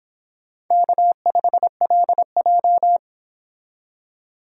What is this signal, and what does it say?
Morse code, heard as a keyed tone, about 26 wpm, 700 Hz. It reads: K5LJ